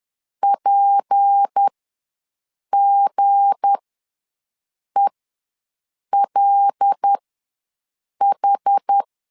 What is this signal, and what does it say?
Morse code, heard as a keyed tone, about 10 wpm, spaced as for 6 wpm, 800 Hz. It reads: PGELH
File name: LEZIONI / 7 GF9Z / 7 G ws 10wpm 3843